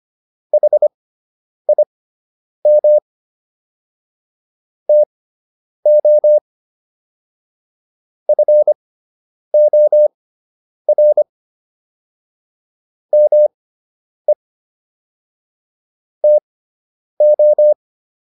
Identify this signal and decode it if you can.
Morse code, heard as a keyed tone, about 25 wpm, spaced as for 9 wpm, 600 Hz. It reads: HIM TO FOR ME TO